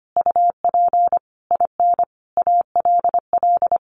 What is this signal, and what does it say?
Morse code, heard as a keyed tone, about 25 wpm, 700 Hz. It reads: UP IN ALL
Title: Sentences from Top 1500 Words 25wpm